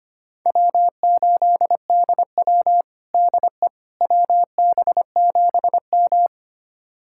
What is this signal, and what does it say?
Morse code, heard as a keyed tone, about 25 wpm, 700 Hz. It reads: W8DW DE WB7M
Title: USA Callsign DE Callsign - 4 Characters - 25wpm